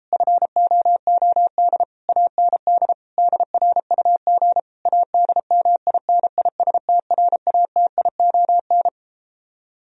Morse code, heard as a keyed tone, about 37 wpm, 700 Hz. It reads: FOOD AND DRUG ADMINISTRATION